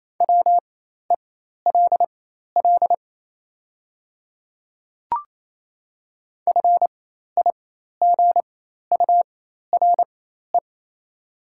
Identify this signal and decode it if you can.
Morse code, heard as a keyed tone, about 28 wpm, spaced as for 13 wpm, 690 Hz. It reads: WELL FIGURE
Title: 4x Character Spacing - Top 500 Words - 28wpm